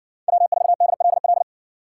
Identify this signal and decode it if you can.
Morse code, heard as a keyed tone, about 55 wpm, 700 Hz. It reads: W4DLL